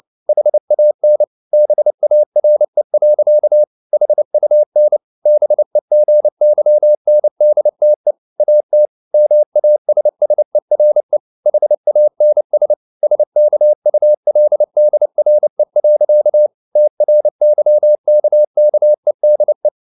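Morse code, heard as a keyed tone, about 29 wpm, 600 Hz. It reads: HAN BARE. HUN BEGYNDTE AT MASSERE HANS SKULDRE. TRYKKEDE
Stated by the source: Fast (QRQ) Morse at 29 wpm, from CW med Gnister.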